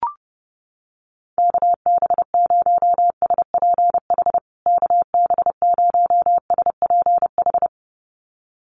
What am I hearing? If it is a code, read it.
Morse code, heard as a keyed tone, about 30 wpm, 700 Hz. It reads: KB0SPH KB0SPH
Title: USA Call Signs Repeated 2x 30wpm